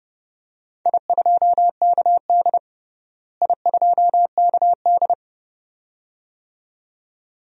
Morse code, heard as a keyed tone, about 30 wpm, 700 Hz. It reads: I2KD I2KD
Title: DXCC-100 Call Signs - 4 Characters - Repeated 2x - 30wpm